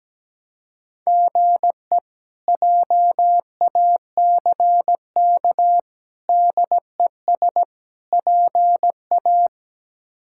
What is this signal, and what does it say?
Morse code, heard as a keyed tone, about 17 wpm, 700 Hz. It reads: GE JACK DES PA
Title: K1USN SST QSO Practice - 17wpm